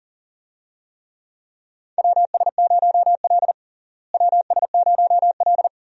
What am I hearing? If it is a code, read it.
Morse code, heard as a keyed tone, about 40 wpm, 700 Hz. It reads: WS0L WS0L
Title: USA Call Signs - 4 Characters - Repeated 3x - 40wpm